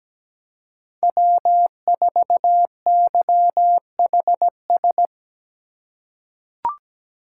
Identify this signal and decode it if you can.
Morse code, heard as a keyed tone, about 17 wpm, 700 Hz. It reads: W4YHS
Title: USA Call Signs - 5 Characters - 2x Character Spacing - 17wpm